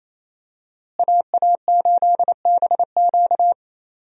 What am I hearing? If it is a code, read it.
Morse code, heard as a keyed tone, about 28 wpm, 700 Hz. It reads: AA8BQ
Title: USA Call Signs - 5 Characters - Repeated 3x - 28wpm